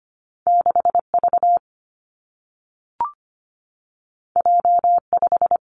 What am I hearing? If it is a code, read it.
Morse code, heard as a keyed tone, about 25 wpm, 700 Hz. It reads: 6V J5
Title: Two Challenging Letter-Number Combinations 25wpm